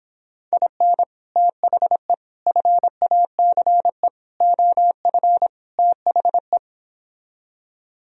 Morse code, heard as a keyed tone, about 29 wpm, 700 Hz. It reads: IN THE FACE OF THE